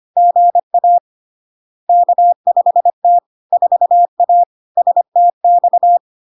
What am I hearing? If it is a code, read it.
Morse code, heard as a keyed tone, about 25 wpm, 700 Hz. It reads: GA K5T 4A STX
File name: Log-in-CW-25wpm-700-Hz